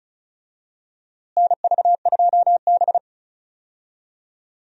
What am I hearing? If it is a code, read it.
Morse code, heard as a keyed tone, about 36 wpm, 700 Hz. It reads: NV2B
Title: USA Call Signs - 4 Characters - Repeated 2x - 35wpm